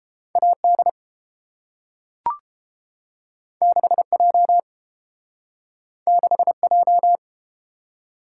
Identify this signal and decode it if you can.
Morse code, heard as a keyed tone, about 30 wpm, 700 Hz. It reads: AD 6J 6J